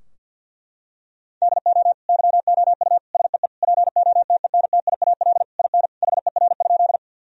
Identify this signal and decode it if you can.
Morse code, heard as a keyed tone, about 50 wpm, 700 Hz. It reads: DO YOU SEE POTENTIAL IN HER?
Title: Sentences from Top 1000 Words - Review Part 4 - 50wpm